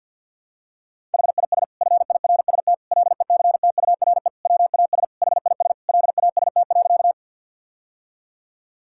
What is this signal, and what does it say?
Morse code, heard as a keyed tone, about 50 wpm, 700 Hz. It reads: HIS FIRST LECTURE WAS HIS LAST.
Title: Sentences from Top 1600 Words 50wpm